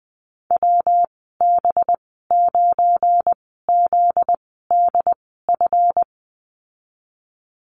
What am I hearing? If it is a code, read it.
Morse code, heard as a keyed tone, about 20 wpm, spaced as for 15 wpm, 700 Hz. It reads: WB9ZDF
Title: 245 Lesson 39g - Letter z - Call Signs 20wpm